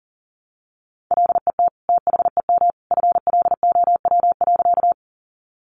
Morse code, heard as a keyed tone, about 40 wpm, 700 Hz. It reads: LET THEM FLOW.
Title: Sentences from Top 1200 Words 40wpm